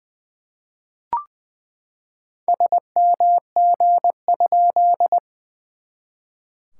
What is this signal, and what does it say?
Morse code, heard as a keyed tone, about 20 wpm, 700 Hz. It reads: SMG?
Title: SOTA QSO Practice 20wpm